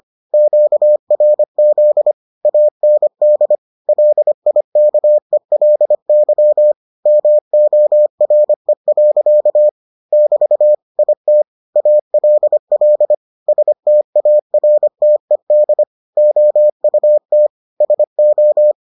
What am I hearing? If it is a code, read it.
Morse code, heard as a keyed tone, about 25 wpm, 600 Hz. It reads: QRZ AND LIKELY MORE. <BT> IT ALL STARTED OUT SO